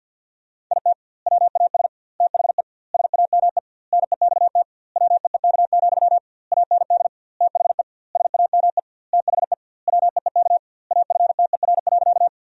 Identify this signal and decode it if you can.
Morse code, heard as a keyed tone, about 50 wpm, 700 Hz. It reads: IT WAS THE SAME NEXT WEEK, AND THE SAME THE WEEK AFTER.
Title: 2x Word-Spacing - Sentences from Top 500 Words - 50wpm